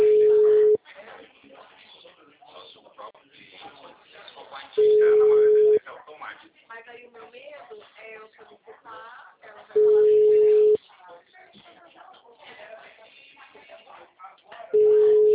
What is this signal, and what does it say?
Ringback tone (Europe, 425 Hz), with speech.
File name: out- -2016- - - .48049